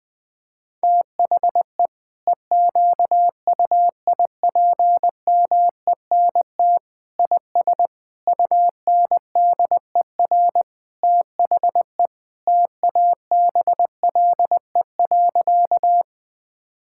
Morse code, heard as a keyed tone, about 21 wpm, 700 Hz. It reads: THE EQUIPMENT IS UNDER THE TABLE.
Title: Sentences from Top 1000 Words - Review Part 5 - 20wpm